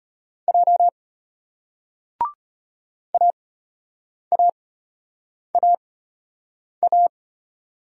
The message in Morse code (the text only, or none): J AAAA